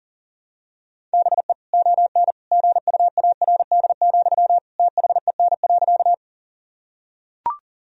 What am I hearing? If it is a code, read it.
Morse code, heard as a keyed tone, about 40 wpm, 700 Hz. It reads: BE ON GUARD, THEN.
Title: Sentences from Top 1000 Words - Review Part 4 - 40wpm